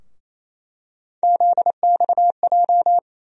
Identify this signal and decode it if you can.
Morse code, heard as a keyed tone, about 28 wpm, 700 Hz. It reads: ZXJ
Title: USA Call Sign Postfixes - 3 Letters 28wpm